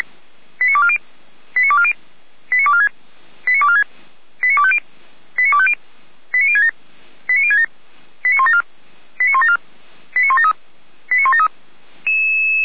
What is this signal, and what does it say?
Five-tone selective call (ZVEI), heard as quick runs of five tones, tones of about 70 ms. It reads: 89240 89240 89246 89246 89250 89250 89067 89067 89174 89174 89173 89173